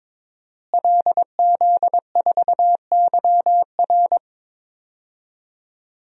Morse code, heard as a keyed tone, about 22 wpm, 700 Hz. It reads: LZ4YR
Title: DXCC-100 Call Signs - 5 Characters - Repeated 3x - 22wpm